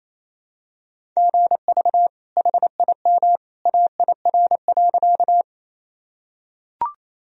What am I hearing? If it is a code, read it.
Morse code, heard as a keyed tone, about 28 wpm, 700 Hz. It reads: GV HIM AIR.
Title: Sentences from Top 200 Words and Abbreviations 28wpm